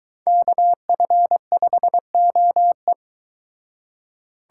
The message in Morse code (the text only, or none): KF5OE